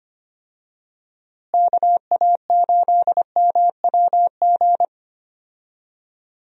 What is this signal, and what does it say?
Morse code, heard as a keyed tone, about 25 wpm, 700 Hz. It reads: KA8MWG
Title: USA Call Signs - 6 Characters - Repeated 2x - 25wpm